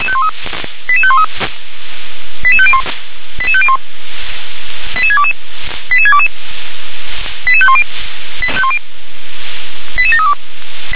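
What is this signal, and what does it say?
Five-tone selective call (ZVEI), heard as quick runs of five tones, tones of about 70 ms. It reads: 88513 88501 88501 88622 88622 88510 88732
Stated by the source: Five-tone ZVEI selective call 88511, received on SDR.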